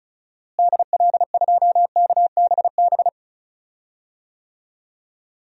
Morse code, heard as a keyed tone, about 34 wpm, 700 Hz. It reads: DL2KBB
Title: DXCC-100 Call Signs - 6 Characters - Repeated 3x - 35wpm